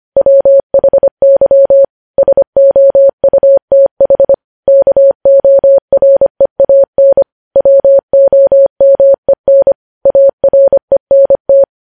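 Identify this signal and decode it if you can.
Morse code, heard as a keyed tone, about 25 wpm, 550 Hz. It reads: WHY SOUTH KOREAN WOMEN ARENT
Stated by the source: News headlines sent in Morse at 25 wpm.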